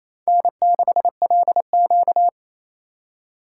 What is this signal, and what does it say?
Morse code, heard as a keyed tone, about 27 wpm, 700 Hz. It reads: N6LQ
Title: USA Call Signs - 4 Characters - 4x Character Spacing - 28wpm